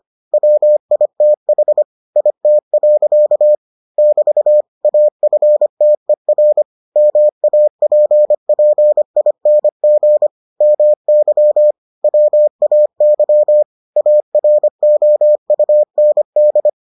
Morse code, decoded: WITH IT. <BT> AFTER MAPPING MY WAY AROUND

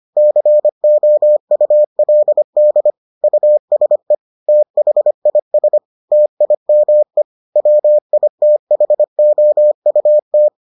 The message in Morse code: COULD USE THIS TIME WITHOUT

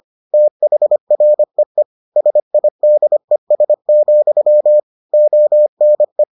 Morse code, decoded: THREE SIDES, ONE

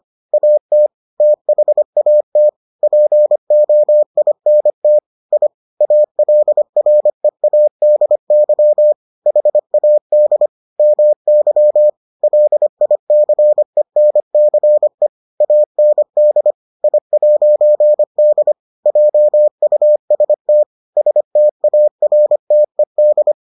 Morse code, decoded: AT THAT POINT I ALREADY HAD MY LICENCE AND I'D JUST STARTED

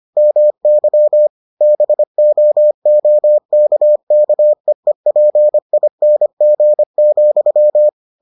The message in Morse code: MY BOOKKEEPING,

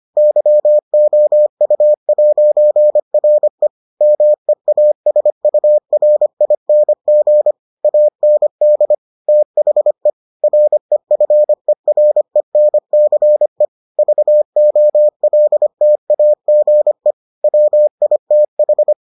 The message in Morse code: YOU'RE MEASURING AND THE REFERENCE VOLTAGE WITH

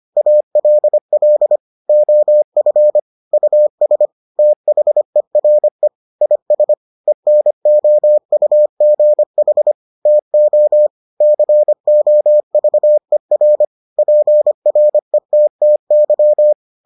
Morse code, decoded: ALL OF US THERE IS ENOUGH TO COVER PRETTY

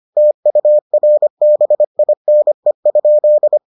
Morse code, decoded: TURBINE?

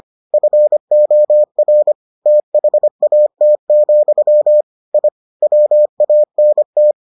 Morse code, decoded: FOR THAT, I WANT

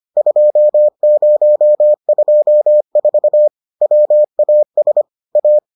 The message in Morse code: 2024 WAS A

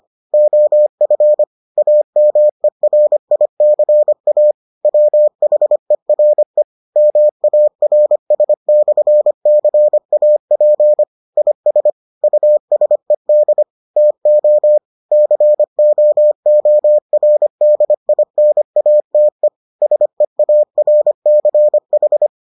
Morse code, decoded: OF AMERICA WHERE MARS/CAP IS USED TO COORDINATE SEARCH

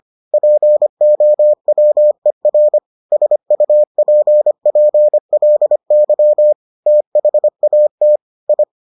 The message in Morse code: POWER SUPPLY THAT I

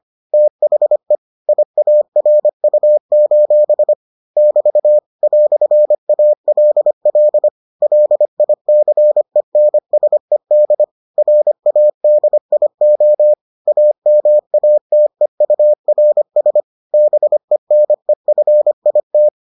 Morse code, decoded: THE IARU: <BT> "ALL LICENSED RADIO AMATEURS BENEFIT

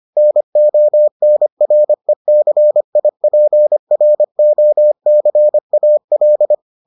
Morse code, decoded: NONRECIPROCAL

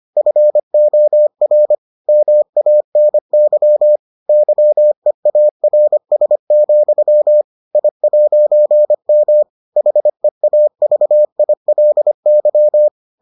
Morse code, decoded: FOR MANY YEARS, I'M HEAVILY

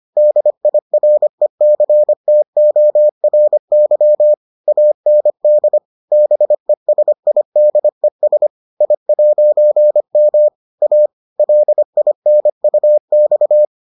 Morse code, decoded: DIRECTORY AND BESIDES I'M A LINUX

